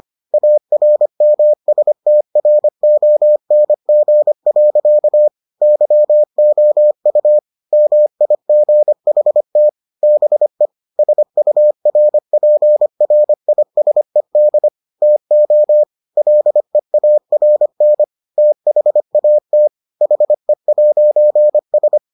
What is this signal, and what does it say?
Morse code, heard as a keyed tone, about 25 wpm, 600 Hz. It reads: ARMSTRONG. YOU MIGHT BE SURPRISED TO LEARN THAT HE'S